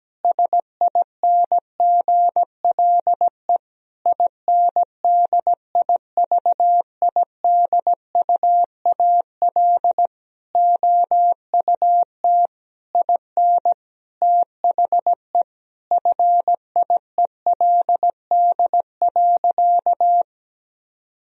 Morse code, heard as a keyed tone, about 17 wpm, 700 Hz. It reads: SINGLE INDIVIDUAL OUT IN THE FIELD.
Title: Sentences from Top 1000 Words - Review Part 4 - 17wpm